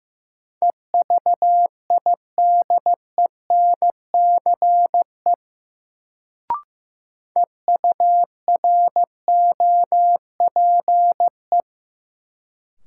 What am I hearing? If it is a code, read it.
Morse code, heard as a keyed tone, about 15 wpm, 700 Hz. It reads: EVIDENCE EUROPE
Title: Sets of 5 Words Encore - New Word Drill - 15wpm